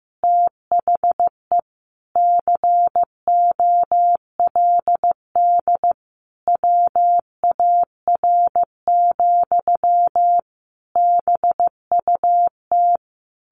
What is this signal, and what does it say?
Morse code, heard as a keyed tone, about 15 wpm, 700 Hz. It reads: THE COLD WAR, BUT
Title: Sentences from Top 1200 Words 15wpm